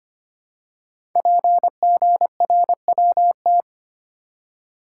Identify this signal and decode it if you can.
Morse code, heard as a keyed tone, about 25 wpm, 700 Hz. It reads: PGRWT
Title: Code Groups - 5 Letters - 25wpm